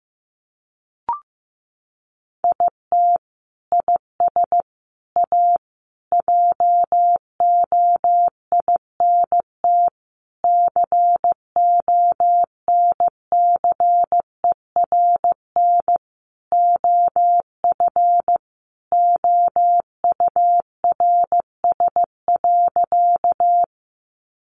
Morse code, heard as a keyed tone, about 15 wpm, 700 Hz. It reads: IT IS A JOINT CONCERN OF OURS.
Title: Sentences from Top 1300 Words 15wpm